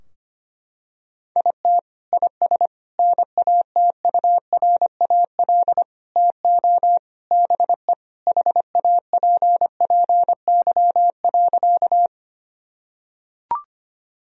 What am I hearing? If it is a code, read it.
Morse code, heard as a keyed tone, about 25 wpm, 700 Hz. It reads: IT IS NATURAL TO BE HAPPY.
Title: Sentences from Top 1000 Words - Review Part 4 - 25wpm